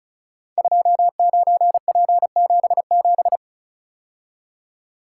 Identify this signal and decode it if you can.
Morse code, heard as a keyed tone, about 36 wpm, 690 Hz. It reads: 29P77
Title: Code Groups - 5 Letter:Numbers - 35wpm